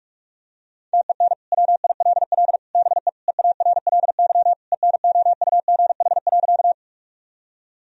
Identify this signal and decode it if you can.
Morse code, heard as a keyed tone, about 46 wpm, 700 Hz. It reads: TEN WILL BE EARLY ENOUGH.
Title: Sentences from Top 1000 Words - Review Part 2 - 45wpm